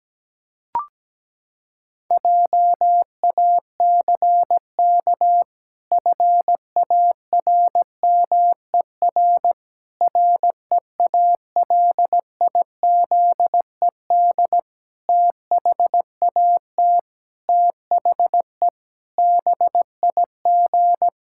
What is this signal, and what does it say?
Morse code, heard as a keyed tone, about 17 wpm, 700 Hz. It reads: JACK FARMER REALIZED THAT THE BIG
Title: Pangram Challenge v2 - 17wpm